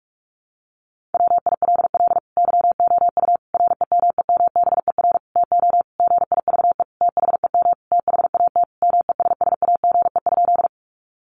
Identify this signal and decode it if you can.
Morse code, heard as a keyed tone, about 44 wpm, 700 Hz. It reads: WILL YOU REMEMBER TO GIVE THEM THAT MESSAGE?
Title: Sentences from Top 1100 Words 45wpm